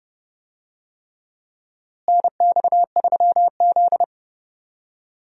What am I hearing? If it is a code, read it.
Morse code, heard as a keyed tone, about 31 wpm, 700 Hz. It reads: NX3Z
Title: USA Call Signs - 4 Characters - Repeated 3x - 30wpm